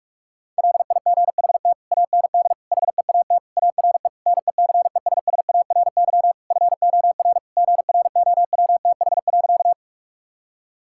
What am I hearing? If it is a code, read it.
Morse code, heard as a keyed tone, about 43 wpm, 700 Hz. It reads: LIGHT AND HEAT ARE NECESSARY FOR GROWTH.
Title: Sentences from Top 1000 Words - Review Part 5 - 45wpm